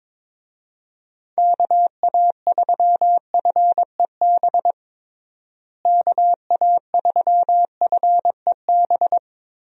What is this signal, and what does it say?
Morse code, heard as a keyed tone, about 22 wpm, 700 Hz. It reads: KA3FEB KA3FEB
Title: USA Call Signs - 6 Characters - Repeated 3x - 22wpm